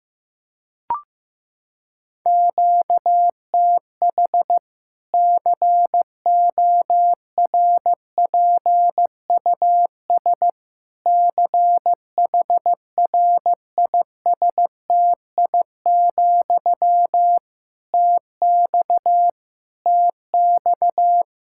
Morse code, heard as a keyed tone, about 16 wpm, 700 Hz. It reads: QTH CORPUS CHRISTI, TX TX
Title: K1USN SST QSO Practice 15wpm